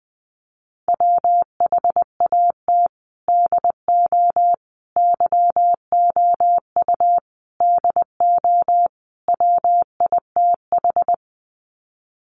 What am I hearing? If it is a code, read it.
Morse code, heard as a keyed tone, about 20 wpm, 700 Hz. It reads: WHAT DO YOU DO WITH
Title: Sets of 5 Words Encore2 20wpm